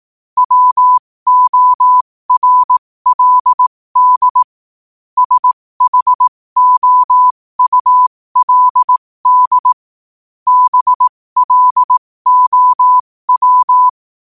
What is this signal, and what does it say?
Morse code, heard as a keyed tone, about 18 wpm, spaced as for 15 wpm, 1000 Hz. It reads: WORLD SHOULD BLOW
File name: qotdfiles / 15wpm